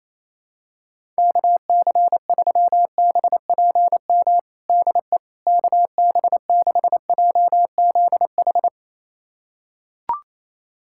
Morse code, heard as a keyed tone, about 28 wpm, 700 Hz. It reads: KC3BPM DE KB6JZH
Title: USA Callsign DE Callsign - 6 Characters - 28wpm